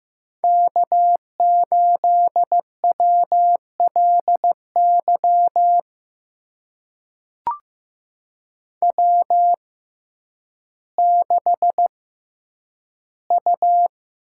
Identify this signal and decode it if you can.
Morse code, heard as a keyed tone, about 15 wpm, spaced as for 9 wpm, 700 Hz. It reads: K8WLY W6U
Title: USA Call Signs - 5 Characters - 6x Character Spacing - 15wpm